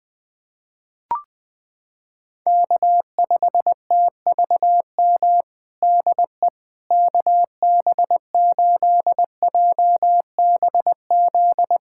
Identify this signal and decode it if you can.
Morse code, heard as a keyed tone, about 20 wpm, 700 Hz. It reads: K5TVM DE KB8JBZ
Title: USA Callsign DE Callsign 20wpm